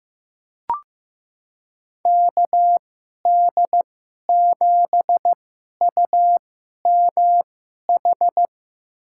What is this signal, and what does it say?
Morse code, heard as a keyed tone, about 15 wpm, spaced as for 11 wpm, 700 Hz. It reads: KD7UMH
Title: USA Call Signs - 6 Characters - 2x Character Spacing - 15wpm